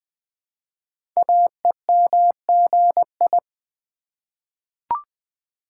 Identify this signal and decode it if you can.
Morse code, heard as a keyed tone, about 20 wpm, 700 Hz. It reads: AEMGI
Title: Code Groups - 5 Letters - 20wpm